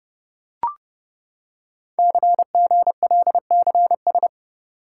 Morse code, heard as a keyed tone, about 30 wpm, 700 Hz. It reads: CGLCS
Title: Code Groups - 5 Letters - 30wpm